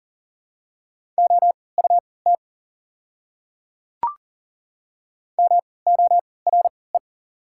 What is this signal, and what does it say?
Morse code, heard as a keyed tone, about 40 wpm, spaced as for 23 wpm, 690 Hz. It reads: OUT MORE
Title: 3x Character Spacing - Top 100 Words - 40wpm